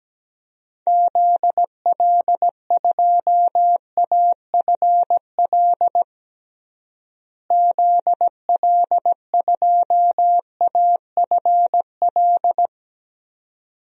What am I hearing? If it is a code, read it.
Morse code, heard as a keyed tone, about 17 wpm, 700 Hz. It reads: ZL2AFL ZL2AFL
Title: DXCC-100 Call Signs - 6 Characters - Repeated 2x - 17wpm